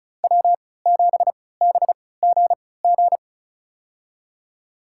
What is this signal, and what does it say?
Morse code, heard as a keyed tone, about 34 wpm, spaced as for 20 wpm, 700 Hz. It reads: W7BGG
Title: USA Call Signs - 5 Characters - 3x Character Spacing - 35wpm